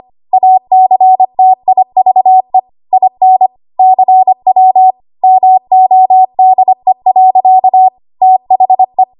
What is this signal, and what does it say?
Morse code, heard as a keyed tone, about 25 wpm, 750 Hz. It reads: ACTIVE IN CW MODE. THE
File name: Archive / 25 WPM / 25WPM